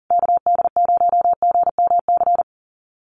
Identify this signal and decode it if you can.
Morse code, heard as a keyed tone, about 40 wpm, 700 Hz. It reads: KD0GMC
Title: USA Call Signs Repeated 3x 40wpm